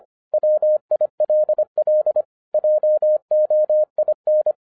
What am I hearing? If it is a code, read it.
Morse code, heard as a keyed tone, about 25 wpm, 610 Hz. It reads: WILL JOIN
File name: 2011-07-mp3 / n24 QRP OTAKU 25wpm